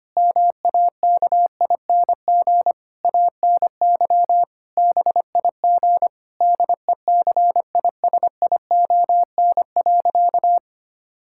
Morse code, heard as a keyed tone, about 25 wpm, 700 Hz. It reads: MAKING ANY BIG DECISION.
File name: practice / 25wpm / Difficult Sentences with Hints - 25wpm